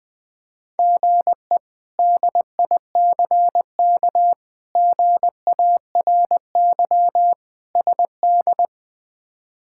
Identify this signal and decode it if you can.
Morse code, heard as a keyed tone, about 20 wpm, 700 Hz. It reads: GE DICK GARY SD
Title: K1USN SST QSO Practice - 20wpm